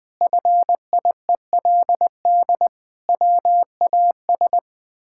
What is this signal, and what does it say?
Morse code, heard as a keyed tone, about 20 wpm, 700 Hz. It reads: FIELD WAS
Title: Sentences with 6-Word Phrases - 20wpm